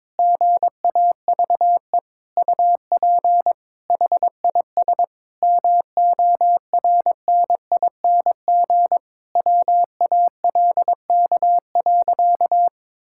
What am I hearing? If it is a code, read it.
Morse code, heard as a keyed tone, about 22 wpm, 700 Hz. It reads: GAVE UP HIS MORNING WALK.
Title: Sentences from Top 1000 Words - Review Part 5 - 22wpm